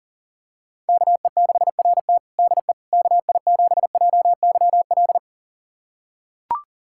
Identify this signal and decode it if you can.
Morse code, heard as a keyed tone, about 40 wpm, 700 Hz. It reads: KE6RT DE KI7JYL
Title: USA Callsign DE Callsign 40wpm